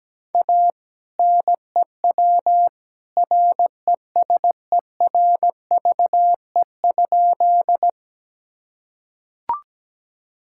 Morse code, heard as a keyed tone, about 17 wpm, 700 Hz. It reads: A NEW RESERVE?
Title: Sentences from Top 1000 Words - Review Part 6 - 17wpm